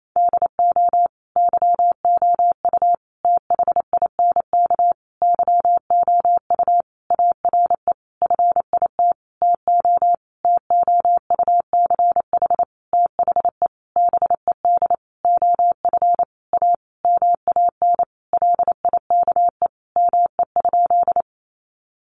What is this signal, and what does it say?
Morse code, heard as a keyed tone, about 28 wpm, 700 Hz. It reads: DO YOU THINK YOU ARE FIT TO TOUCH THE BED OF A MAN LIKE ME?
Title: Sentences from Top 1200 Words 28wpm